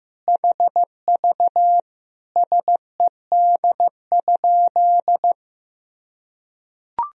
Morse code, heard as a keyed tone, about 15 wpm, 700 Hz. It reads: HV SED?
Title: Sentences from Top 100 Words and Abbreviations 15wpm